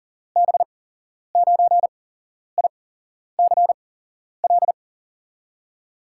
Morse code, heard as a keyed tone, about 39 wpm, spaced as for 11 wpm, 700 Hz. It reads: B9ICL
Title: USA Call Signs - 6 Characters - 8x Character Spacing - 40wpm